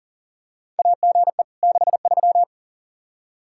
Morse code, heard as a keyed tone, about 40 wpm, 700 Hz. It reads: AGE 63
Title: Practice QSO - Exchange 3 - More Info - 40wpm